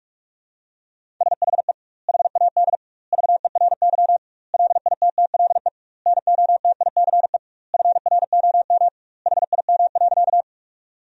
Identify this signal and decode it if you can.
Morse code, heard as a keyed tone, about 41 wpm, 700 Hz. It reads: SHE HAD VERY LITTLE NOTICE FROM HIM.
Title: 2x Word-Spacing - Sentences from Top 400 Words - 45wpm